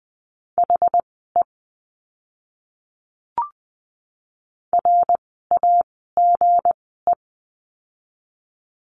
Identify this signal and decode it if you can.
Morse code, heard as a keyed tone, about 20 wpm, spaced as for 14 wpm, 700 Hz. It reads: HE RAGE
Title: 160 Lesson 26f - Letter g - Words 20wpm